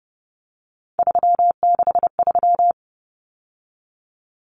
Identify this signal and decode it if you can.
Morse code, heard as a keyed tone, about 30 wpm, 700 Hz. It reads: 363
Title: Common Cut Numbers - 3-Digit 30wpm